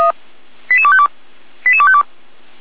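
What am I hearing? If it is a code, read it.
Five-tone selective call (ZVEI), heard as quick runs of five tones, tones of about 70 ms. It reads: 88252 88252
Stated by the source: ZVEI five-tone selective call received on SDR.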